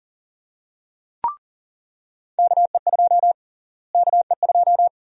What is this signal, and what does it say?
Morse code, heard as a keyed tone, about 38 wpm, 700 Hz. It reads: KE2 KE2
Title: USA Call Sign Prefixes - Repeated 2x - 40wpm